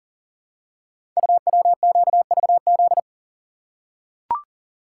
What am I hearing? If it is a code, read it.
Morse code, heard as a keyed tone, about 38 wpm, 700 Hz. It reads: UWQVZ
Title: Code Groups - 5 Letters - 40wpm